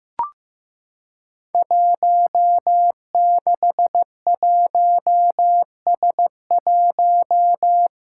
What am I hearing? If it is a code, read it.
Morse code, heard as a keyed tone, about 15 wpm, 700 Hz. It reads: 161S1
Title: Five Challenging Letter-Number Combinations 15wpm